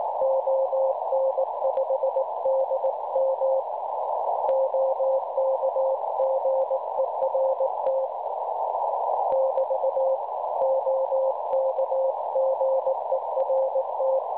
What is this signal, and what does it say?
Morse code, heard as a keyed tone, about 18 wpm, 530 Hz, with noise. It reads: ON5DM OK GERT <BT> OK GERT
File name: cwronde / websdr recording .6kHz